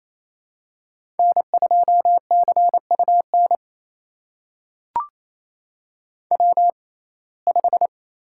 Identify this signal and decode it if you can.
Morse code, heard as a keyed tone, about 28 wpm, spaced as for 20 wpm, 700 Hz. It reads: N2CUN W5
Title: USA Call Signs - 6x Character Spacing - 28wpm